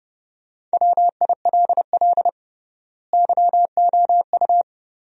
Morse code, heard as a keyed tone, about 30 wpm, 700 Hz. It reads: WILL YOU
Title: 3x Word-Spacing - Sentences from Top 200 Words - 30wpm